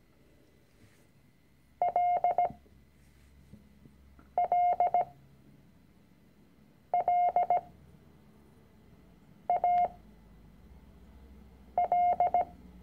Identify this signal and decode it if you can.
Morse code, heard as a keyed tone, about 17 wpm, spaced as for 4 wpm, 710 Hz. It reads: LLLAL
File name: CW-Online-Kurs / Anfänger / Lektion 006